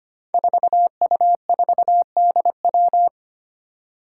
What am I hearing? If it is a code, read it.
Morse code, heard as a keyed tone, about 25 wpm, 700 Hz. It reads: 4U4DW
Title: Five Challenging Letter-Number Combinations 25wpm